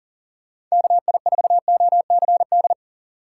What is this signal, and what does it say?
Morse code, heard as a keyed tone, about 40 wpm, 700 Hz. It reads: KI4OCD
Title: USA Call Signs - 6 Characters - 2x Character Spacing - 40wpm